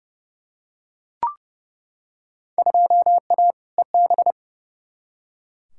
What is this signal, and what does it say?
Morse code, heard as a keyed tone, about 30 wpm, 700 Hz. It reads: 2A EB
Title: Field Day Exchanges - 30wpm